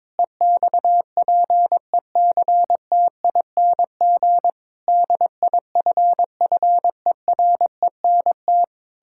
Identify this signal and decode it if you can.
Morse code, heard as a keyed tone, about 22 wpm, 700 Hz. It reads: EXPECTING DIFFERENT
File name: Sentences with 9-Word Phrases - 22wpm